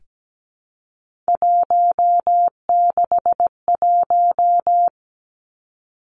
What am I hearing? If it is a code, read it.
Morse code, heard as a keyed tone, about 17 wpm, 700 Hz. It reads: 161
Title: Three Challenging Letter-Number Combinations 17wpm